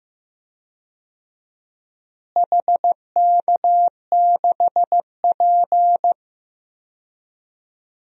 Morse code, heard as a keyed tone, about 15 wpm, 700 Hz. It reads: HK6P
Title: DXCC-100 Call Signs - 4 Characters - Repeated 3x - 15wpm